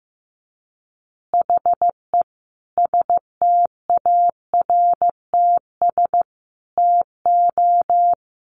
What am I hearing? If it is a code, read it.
Morse code, heard as a keyed tone, about 15 wpm, 700 Hz. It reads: HE STARTS TO
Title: Additional Easy Phrases 15wpm